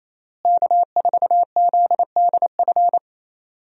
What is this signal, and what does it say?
Morse code, heard as a keyed tone, about 28 wpm, 700 Hz. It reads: K4ZDF
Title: USA Call Signs - 5 Characters - Repeated 3x - 28wpm